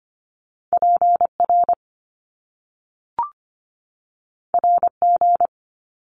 Morse code, heard as a keyed tone, about 25 wpm, 700 Hz. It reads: PR RG
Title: Top Two Letter Combinations in Words 25wpm